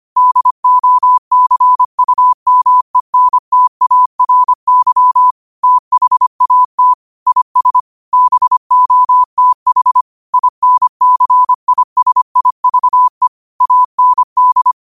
Morse code, heard as a keyed tone, about 25 wpm, 1000 Hz. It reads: DOCUMENTARY THAT IS BOTH INCISIVE AND
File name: qotdfiles / 25wpm